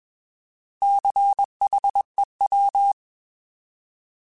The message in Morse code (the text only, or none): CHEW